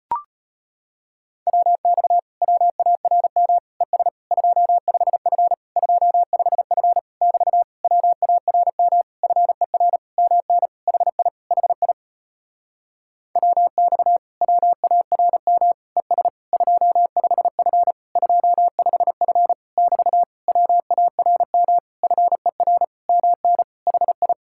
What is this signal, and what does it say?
Morse code, heard as a keyed tone, about 38 wpm, 700 Hz. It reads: WX WARM ES 25F 25F <BT> WARM FER MN HI HI WX WARM ES 25F 25F <BT> WARM FER MN HI